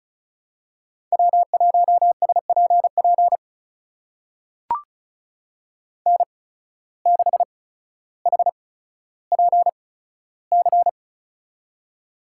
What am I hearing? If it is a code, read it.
Morse code, heard as a keyed tone, about 36 wpm, spaced as for 15 wpm, 700 Hz. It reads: W1SPP N6HPC
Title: USA Call Signs - 5 Characters - 8x Character Spacing - 35wpm